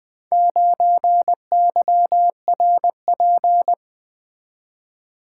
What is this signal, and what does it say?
Morse code, heard as a keyed tone, about 24 wpm, 700 Hz. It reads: 9YRP
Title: Speed-Racing - Four Letter-Number Combinations - 20wpm